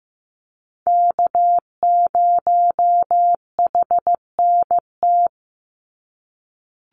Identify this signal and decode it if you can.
Morse code, heard as a keyed tone, about 15 wpm, 700 Hz. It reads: K0HNT
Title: USA Call Signs 15wpm